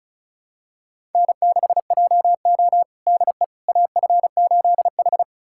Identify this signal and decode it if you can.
Morse code, heard as a keyed tone, about 34 wpm, 700 Hz. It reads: N6JO DE AF8H
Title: USA Callsign DE Callsign - 4 Characters - 35wpm